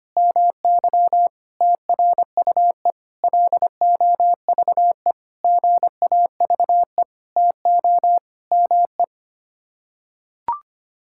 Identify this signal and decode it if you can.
Morse code, heard as a keyed tone, about 25 wpm, 700 Hz. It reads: MY TRUE LOVE GAVE TO ME
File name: Sets of 9 Words - 25wpm